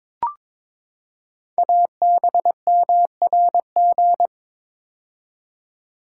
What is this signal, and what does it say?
Morse code, heard as a keyed tone, about 23 wpm, 700 Hz. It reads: ABMRG
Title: Code Groups - 5 Letters - 22wpm